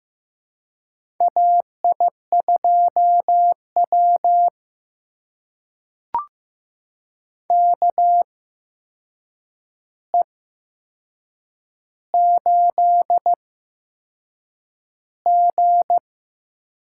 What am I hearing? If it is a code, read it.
Morse code, heard as a keyed tone, about 15 wpm, spaced as for 6 wpm, 700 Hz. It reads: AI2W KE8G
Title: USA Call Signs - 4 Characters - 8x Character Spacing - 15wpm